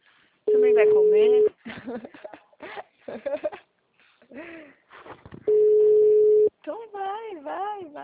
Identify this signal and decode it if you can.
Ringback tone (Europe, 425 Hz), with speech.